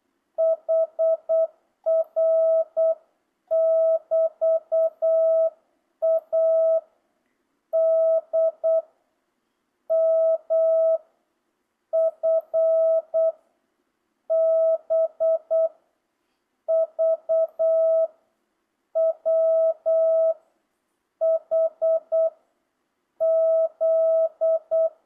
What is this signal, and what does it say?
Morse code, heard as a keyed tone, about 8 wpm, 650 Hz. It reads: HR <BT> A D M F B V W H Z